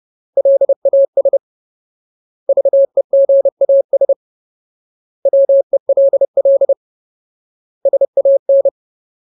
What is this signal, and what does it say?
Morse code, heard as a keyed tone, about 30 wpm, 550 Hz. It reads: LAS VEGAS WELL SAN
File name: Ultimate Word List / 30WPM Ultimate